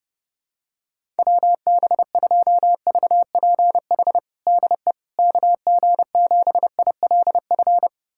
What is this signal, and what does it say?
Morse code, heard as a keyed tone, about 31 wpm, 700 Hz. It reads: WB2VPH DE KG7ILF
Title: USA Callsign DE Callsign 30wpm